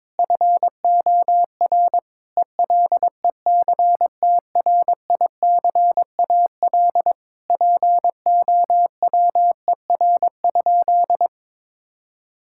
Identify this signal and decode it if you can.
Morse code, heard as a keyed tone, about 22 wpm, 700 Hz. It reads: FOR ELECTRICAL POWER?
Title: Sentences from Top 1000 Words - Review Part 6 - 22wpm